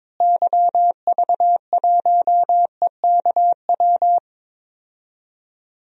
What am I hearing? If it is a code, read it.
Morse code, heard as a keyed tone, about 22 wpm, 700 Hz. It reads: YV1EKW
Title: DXCC-100 Call Signs - 6 Characters - Repeated 3x - 22wpm